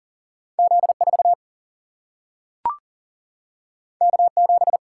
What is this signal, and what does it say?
Morse code, heard as a keyed tone, about 38 wpm, 700 Hz. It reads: Z4 K7
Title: Single-Letter Single-Number 40wpm